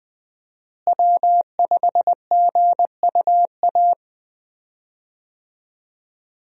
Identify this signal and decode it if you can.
Morse code, heard as a keyed tone, about 20 wpm, 700 Hz. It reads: W5GUA
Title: USA Call Signs - 5 Characters - Repeated 2x - 20wpm